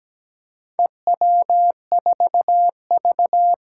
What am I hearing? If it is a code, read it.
Morse code, heard as a keyed tone, about 17 wpm, 700 Hz. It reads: EW4V
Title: Four Challenging Letter-Number Combinations 17wpm